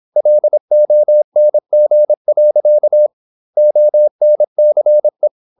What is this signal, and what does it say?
Morse code, heard as a keyed tone, about 26 wpm, 600 Hz. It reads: LONG. ONCE
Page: Never 26wpm | CW med Gnister